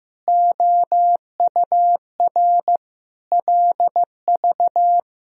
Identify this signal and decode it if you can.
Morse code, heard as a keyed tone, about 15 wpm, 700 Hz. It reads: OUR LV
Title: Sentences from Top 400 Words and Abbreviations 15wpm